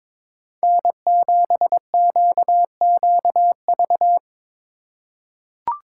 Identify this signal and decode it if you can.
Morse code, heard as a keyed tone, about 22 wpm, 700 Hz. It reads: N7QQV
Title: USA Call Signs - 4x Character Spacing - 22wpm